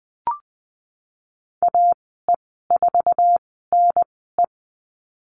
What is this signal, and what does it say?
Morse code, heard as a keyed tone, about 20 wpm, spaced as for 15 wpm, 700 Hz. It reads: AE4NE
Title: 055 Lesson 10g - Letter r - Call Signs 20wpm